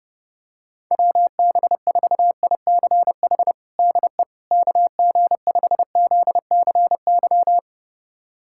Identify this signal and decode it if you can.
Morse code, heard as a keyed tone, about 32 wpm, 700 Hz. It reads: WB4ICH DE KG5ZCY
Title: USA Callsign DE Callsign - 6 Characters - 30wpm